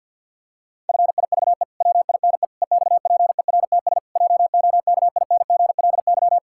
Morse code, heard as a keyed tone, about 47 wpm, 700 Hz. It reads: FIVE WINE EXPERTS JOKINGLY
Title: Pangram Challenge v2 - 50wpm